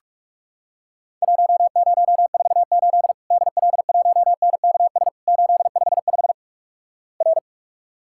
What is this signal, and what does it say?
Morse code, heard as a keyed tone, about 46 wpm, 700 Hz. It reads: 1048 DL1NKS 855 R
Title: Worked All Europe DX Contest - QTC Exchange - 45wpm